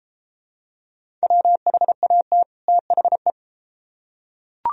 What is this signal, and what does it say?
Morse code, heard as a keyed tone, about 30 wpm, 700 Hz. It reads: WHAT THE